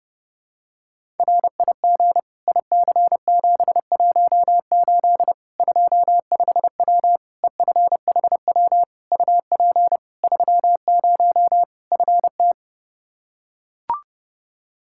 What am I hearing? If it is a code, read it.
Morse code, heard as a keyed tone, about 30 wpm, 700 Hz. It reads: RIG IC718 25W EFHW UP 30 FT
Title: Practice QSO - Exchange 2 - Info - 30wpm